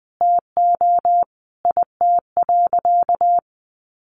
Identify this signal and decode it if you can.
Morse code, heard as a keyed tone, about 20 wpm, 700 Hz. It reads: TO IT.
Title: Sentences from Top 1200 Words 20wpm